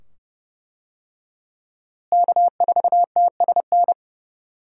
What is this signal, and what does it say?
Morse code, heard as a keyed tone, about 32 wpm, 700 Hz. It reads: K4TSN
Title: USA Call Signs - 5 Characters - Repeated 2x - 30wpm